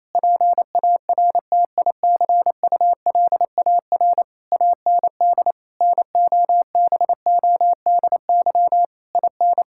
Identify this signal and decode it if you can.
Morse code, heard as a keyed tone, about 28 wpm, 700 Hz. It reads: PARTICULAR AND NOBODY IN
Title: Sentences from Top 1600 Words 28wpm